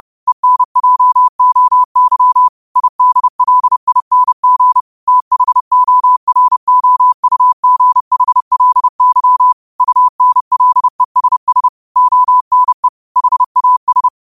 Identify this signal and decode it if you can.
Morse code, heard as a keyed tone, about 30 wpm, 1000 Hz. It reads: ENJOY IDLING THOROUGHLY UNLESS ONE HAS
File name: qotdfiles / 30wpm